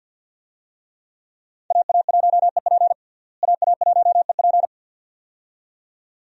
Morse code, heard as a keyed tone, about 49 wpm, 700 Hz. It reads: AA1EP AA1EP
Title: USA Call Signs - 5 Characters - Repeated 2x - 50wpm